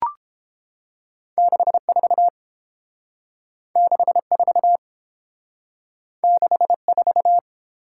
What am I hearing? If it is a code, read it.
Morse code, heard as a keyed tone, about 29 wpm, 700 Hz. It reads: 64 64 64